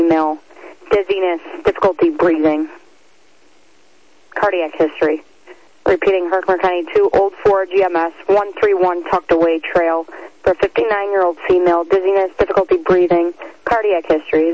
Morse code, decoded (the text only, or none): none